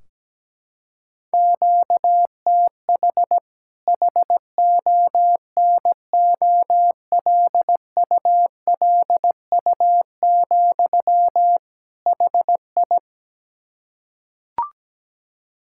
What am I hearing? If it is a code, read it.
Morse code, heard as a keyed tone, about 17 wpm, 700 Hz. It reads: QTH HONOLULU, HI
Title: K1USN SST QSO Practice 17wpm